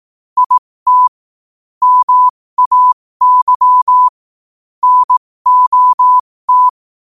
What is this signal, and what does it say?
Morse code, heard as a keyed tone, about 18 wpm, spaced as for 15 wpm, 1000 Hz. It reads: IT MAY NOT
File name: qotdfiles / 15wpm